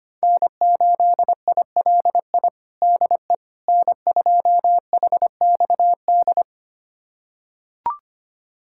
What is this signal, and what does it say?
Morse code, heard as a keyed tone, about 25 wpm, 700 Hz. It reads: N8ILI DE N2HXD
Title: USA Callsign DE Callsign - 5 Characters - 25wpm